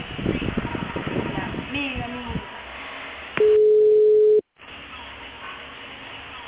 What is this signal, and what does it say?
Ringback tone (Europe, 425 Hz), with speech.